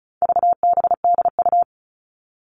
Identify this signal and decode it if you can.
Morse code, heard as a keyed tone, about 34 wpm, 700 Hz. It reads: VBDU
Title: All Cut Numbers - 4-Digit 35wpm